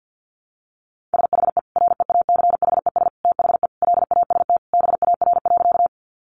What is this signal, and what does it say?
Morse code, heard as a keyed tone, about 50 wpm, 700 Hz. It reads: SHE REACHES THE LAST BAR.
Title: Easy Phrases 50wpm